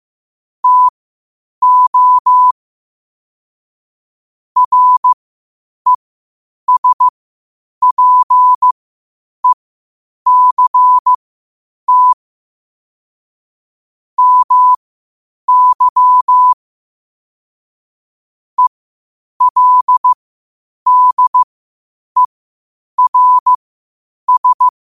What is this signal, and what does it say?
Morse code, heard as a keyed tone, about 15 wpm, spaced as for 8 wpm, 1000 Hz. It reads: TO RESPECT MY ELDERS
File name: qotdfiles / 8wpm